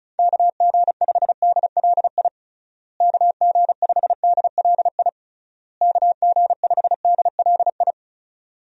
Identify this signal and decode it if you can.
Morse code, heard as a keyed tone, about 34 wpm, 700 Hz. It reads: KG5DLI KG5DLI KG5DLI
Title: USA Call Signs - 6 Characters - Repeated 3x - 35wpm